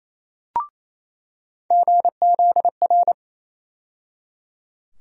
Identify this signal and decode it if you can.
Morse code, heard as a keyed tone, about 28 wpm, 700 Hz. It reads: GZR
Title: USA Call Sign Postfixes - 3 Letters 28wpm